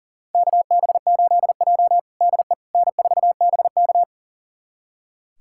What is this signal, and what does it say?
Morse code, heard as a keyed tone, about 40 wpm, 700 Hz. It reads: KB8J DE N4BK
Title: USA Callsign DE Callsign - 4 Characters - 40wpm